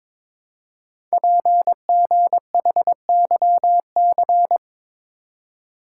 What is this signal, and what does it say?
Morse code, heard as a keyed tone, about 22 wpm, 700 Hz. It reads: PGHYC
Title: Code Groups - 5 Letters - 22wpm